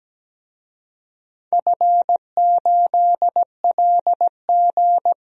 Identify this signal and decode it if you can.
Morse code, heard as a keyed tone, about 17 wpm, 700 Hz. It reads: F8LG
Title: DXCC-100 Call Signs - 4 Characters - Repeated 2x - 17wpm